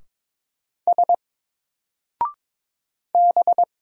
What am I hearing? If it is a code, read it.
Morse code, heard as a keyed tone, about 22 wpm, 700 Hz. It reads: S B
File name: Single-Letter-Number-22wpm